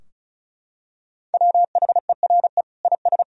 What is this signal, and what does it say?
Morse code, heard as a keyed tone, about 36 wpm, 700 Hz. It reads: WHERE IS
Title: Sentences from Top 1000 Words - Review Part 4 - 35wpm